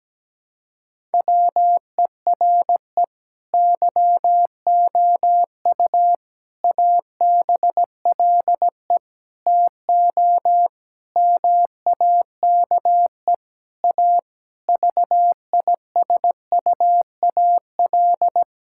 Morse code, heard as a keyed tone, about 17 wpm, 700 Hz. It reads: WERE YOU ABLE TO MAKE A VISUAL
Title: Sentences from Top 1000 Words - Review Part 6 - 17wpm